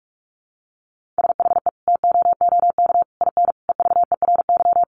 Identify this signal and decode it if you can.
Morse code, heard as a keyed tone, about 44 wpm, 700 Hz. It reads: SHE TOOK IN EVERY
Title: Sentences from Top 1500 Words 45wpm